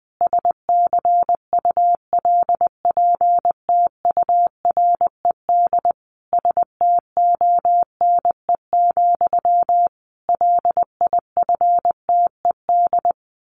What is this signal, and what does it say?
Morse code, heard as a keyed tone, about 20 wpm, 700 Hz. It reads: SCULPTURED STONE, LIFTED
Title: A Sound Of Thunder 20wpm